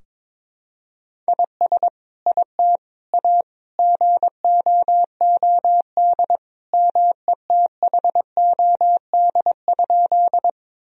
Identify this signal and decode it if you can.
Morse code, heard as a keyed tone, about 22 wpm, 700 Hz. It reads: IS IT A GOOD METHOD?
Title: Sentences from Top 1000 Words - Review Part 3 - 22wpm